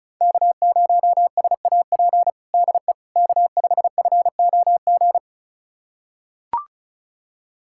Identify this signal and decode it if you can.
Morse code, heard as a keyed tone, about 34 wpm, 700 Hz. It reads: K0SAP DE K5FOG
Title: USA Callsign DE Callsign - 5 Characters - 35wpm